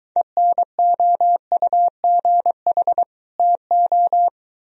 Morse code, heard as a keyed tone, about 23 wpm, 700 Hz. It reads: ENOUGH TO